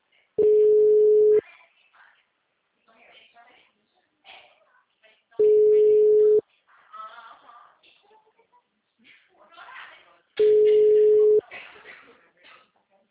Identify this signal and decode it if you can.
Ringback tone (Europe, 425 Hz), with speech.
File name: out- -2004- - - .148